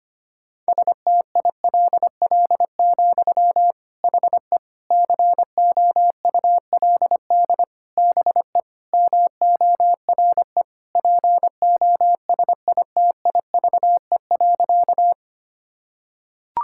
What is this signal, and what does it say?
Morse code, heard as a keyed tone, about 25 wpm, 700 Hz. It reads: STILL, HE COULD BE MORE POSITIVE.
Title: Sentences from Top 1000 Words - Review Part 5 - 25wpm